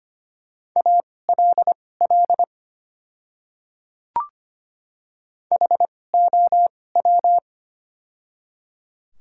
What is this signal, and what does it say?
Morse code, heard as a keyed tone, about 25 wpm, spaced as for 18 wpm, 690 Hz. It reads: ALL HOW
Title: 2x Character Spacing - Top 100 Words - 25wpm